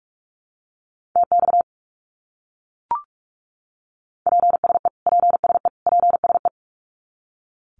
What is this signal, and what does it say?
Morse code, heard as a keyed tone, about 44 wpm, 700 Hz. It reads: TX PSE PSE PSE
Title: Abbreviations Repeated 3x - 45wpm